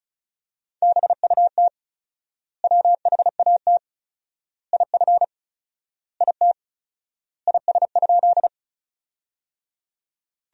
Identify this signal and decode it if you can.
Morse code, heard as a keyed tone, about 34 wpm, 700 Hz. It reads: BUT WHAT IF IT IS?
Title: 4x Word-Spacing - Sentences from Top 100 Words - 35wpm